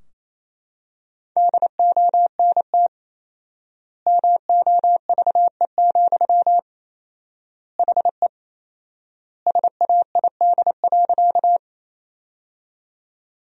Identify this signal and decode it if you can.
Morse code, heard as a keyed tone, about 28 wpm, 700 Hz. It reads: DONT MOVE, HE SAID.
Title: 4x Word-Spacing - Sentences from Top 300 Words - 28wpm